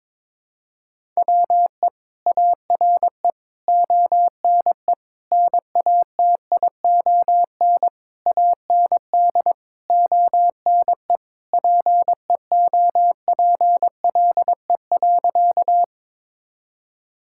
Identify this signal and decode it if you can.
Morse code, heard as a keyed tone, about 23 wpm, 700 Hz. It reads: WE ARE ONE NATION AND ONE PEOPLE.
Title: Sentences from Top 1000 Words - Review Part 6 - 22wpm